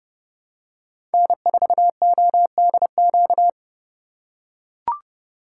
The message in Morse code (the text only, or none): N4ODQ